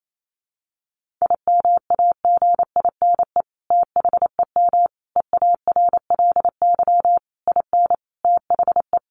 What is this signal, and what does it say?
Morse code, heard as a keyed tone, about 28 wpm, 700 Hz. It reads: IMAGINE THEM EARLY IN THE
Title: Sentences from Top 1400 Words 28wpm